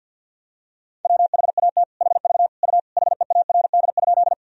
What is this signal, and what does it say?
Morse code, heard as a keyed tone, about 51 wpm, 700 Hz. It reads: WHAT HV U HEARD?
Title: Sentences from Top 400 Words and Abbreviations 50wpm